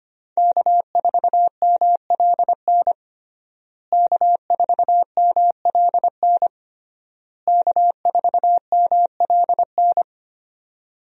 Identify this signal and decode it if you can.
Morse code, heard as a keyed tone, about 25 wpm, 700 Hz. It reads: K4MLN K4MLN K4MLN
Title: USA Call Signs - 5 Characters - Repeated 3x - 25wpm